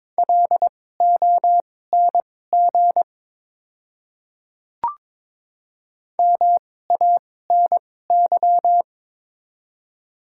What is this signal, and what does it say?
Morse code, heard as a keyed tone, about 22 wpm, spaced as for 16 wpm, 700 Hz. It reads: LONG MANY
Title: 2x Character Spacing - Top 300 Words - 22wpm